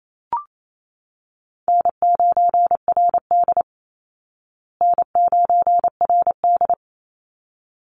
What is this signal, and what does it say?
Morse code, heard as a keyed tone, about 28 wpm, 700 Hz. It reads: N9RD N9RD
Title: Easy Call Signs Repeated 2x - 28wpm